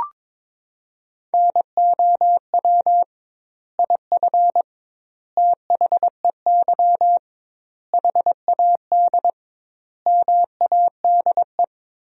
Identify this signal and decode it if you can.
Morse code, heard as a keyed tone, about 22 wpm, 700 Hz. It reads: NOW IF THEY HAD MADE
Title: 2x Word-Spacing - Sentences from Top 500 Words - 22wpm